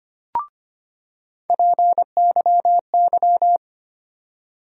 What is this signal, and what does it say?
Morse code, heard as a keyed tone, about 25 wpm, 700 Hz. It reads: PYY
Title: USA Call Sign Postfixes - 3 Letters 25wpm